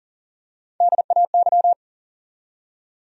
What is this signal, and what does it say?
Morse code, heard as a keyed tone, about 40 wpm, 700 Hz. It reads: DAY
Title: Top 100 Words 40wpm